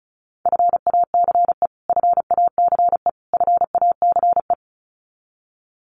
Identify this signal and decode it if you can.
Morse code, heard as a keyed tone, about 34 wpm, 700 Hz. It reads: FACE FACE FACE
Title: Top 500 Words Repeated 35wpm